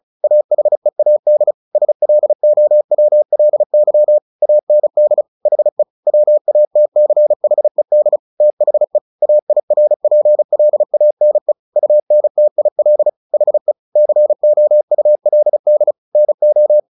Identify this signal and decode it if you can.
Morse code, heard as a keyed tone, about 34 wpm, 600 Hz. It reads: AHEAD SLOWLY AND HE WATCHED THE AIRPLANE UNTIL HE COULD NO
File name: the old man and the sea 2 35wpm